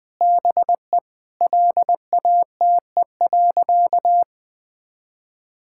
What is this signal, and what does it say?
Morse code, heard as a keyed tone, about 20 wpm, 700 Hz. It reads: BE LATE.
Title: Sentences with 5-Word Phrases - 20wpm